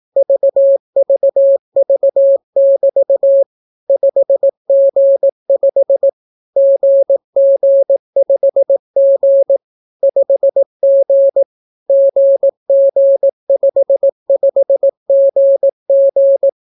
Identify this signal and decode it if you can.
Morse code, heard as a keyed tone, about 18 wpm, 550 Hz. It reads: VVV<BT> 5G5 GG5G 5G GG55GG